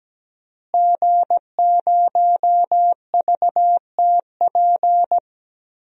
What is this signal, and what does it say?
Morse code, heard as a keyed tone, about 17 wpm, 700 Hz. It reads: G0VTP
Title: DXCC-100 Call Signs - 5 Characters - Repeated 3x - 17wpm